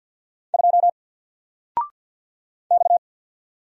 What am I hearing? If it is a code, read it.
Morse code, heard as a keyed tone, about 50 wpm, 700 Hz. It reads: 2 X
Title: Single Character (Letter:Number:Punctuation:Prosign) 50wpm